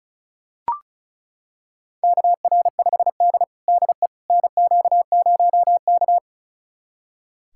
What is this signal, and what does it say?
Morse code, heard as a keyed tone, about 36 wpm, 700 Hz. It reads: KR5D DE NQ0K
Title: USA Callsign DE Callsign - 4 Characters - 35wpm